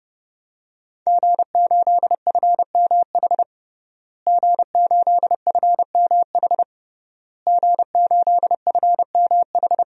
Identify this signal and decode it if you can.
Morse code, heard as a keyed tone, about 30 wpm, 700 Hz. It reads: G8FMH G8FMH G8FMH
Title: DXCC-100 Call Signs - 5 Characters - Repeated 3x - 30wpm